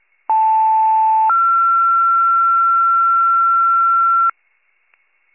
Two-tone page, heard as a one-second tone, then a three-second tone, about 880 then 1360 Hz. It is heard from 0.3 to 4.3 s.